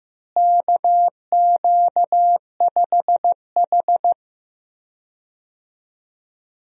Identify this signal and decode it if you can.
Morse code, heard as a keyed tone, about 15 wpm, 700 Hz. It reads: KQ5H
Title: USA Call Signs - 4 Characters - Repeated 3x - 15wpm